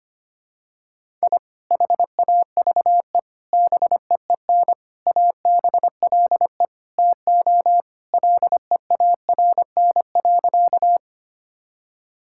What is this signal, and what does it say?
Morse code, heard as a keyed tone, about 25 wpm, 700 Hz. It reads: I HAVE BEEN ABLE TO LEARN.
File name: practice / 25wpm / Sentences with 3-Word Phrases - 25wpm